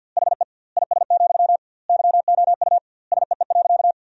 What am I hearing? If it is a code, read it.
Morse code, heard as a keyed tone, about 50 wpm, 700 Hz. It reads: HE IS, YOU SEE.
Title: 2x Word-Spacing - Sentences from Top 100 Words - 50wpm